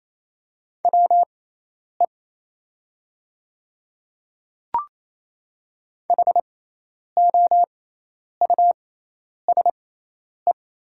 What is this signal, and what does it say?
Morse code, heard as a keyed tone, about 28 wpm, spaced as for 10 wpm, 700 Hz. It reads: WE HOUSE
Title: 6x Character Spacing - Top 200 Words - 28wpm